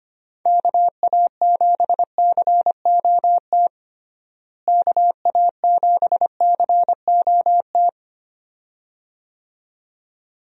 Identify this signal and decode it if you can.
Morse code, heard as a keyed tone, about 25 wpm, 700 Hz. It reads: KA7COT KA7COT
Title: USA Call Signs - 6 Characters - Repeated 2x - 25wpm